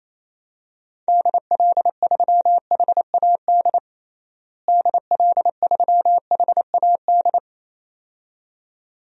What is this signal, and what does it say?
Morse code, heard as a keyed tone, about 28 wpm, 700 Hz. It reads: DL3HAD DL3HAD
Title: DXCC-100 Call Signs - 6 Characters - Repeated 2x - 28wpm